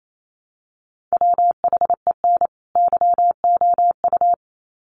Morse code, heard as a keyed tone, about 28 wpm, 700 Hz. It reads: WHEN YOU
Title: Sets of 2 Words 28wpm